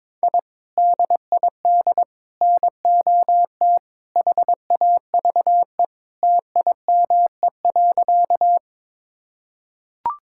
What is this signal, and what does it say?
Morse code, heard as a keyed tone, about 22 wpm, 700 Hz. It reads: I DID NOT HAVE TIME.
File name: practice / 22wpm / Sentences with 3-Word Phrases - 22wpm